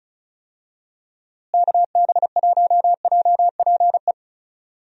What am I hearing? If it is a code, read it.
Morse code, heard as a keyed tone, about 36 wpm, 700 Hz. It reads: KB1JPE
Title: USA Call Signs - 6 Characters - Repeated 3x - 35wpm